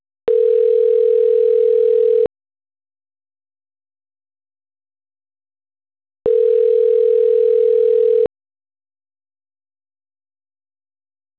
Ringback tone (North America), 440 + 480 Hz.
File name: autotestagent / 11-02-01.136..callcentertransfer. .57f3ae21-50e9-4b2a-9bf8